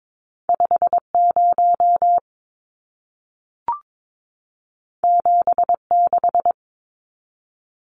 Morse code, heard as a keyed tone, about 22 wpm, 700 Hz. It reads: 50 76
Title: Numbers - 2-Digit 22wpm